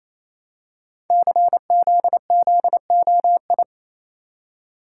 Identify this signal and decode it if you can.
Morse code, heard as a keyed tone, about 27 wpm, 700 Hz. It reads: CZZOI
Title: Code Groups - 5 Letters - 28wpm